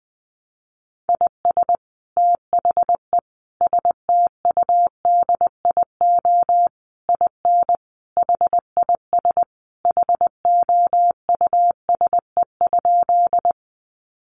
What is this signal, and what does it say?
Morse code, heard as a keyed tone, about 20 wpm, 700 Hz. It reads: IS THE STUDIO IN HIS HOUSE?
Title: Sentences from Top 1200 Words 20wpm